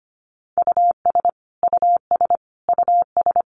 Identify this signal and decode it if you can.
Morse code, heard as a keyed tone, about 25 wpm, 700 Hz. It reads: US US US
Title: Top 500 Words Repeated 25wpm